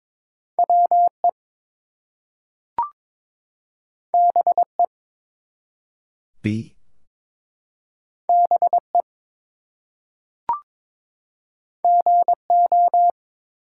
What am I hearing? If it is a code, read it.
Morse code, heard as a keyed tone, about 22 wpm, 700 Hz. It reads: WE BE BE GO
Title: Two Letter Words - 22wpm